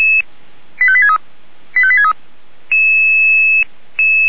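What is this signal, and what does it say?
Five-tone selective call (ZVEI), heard as quick runs of five tones, tones of about 70 ms. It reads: 85863 85863
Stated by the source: ZVEI five-tone call 85863 received on an SDR.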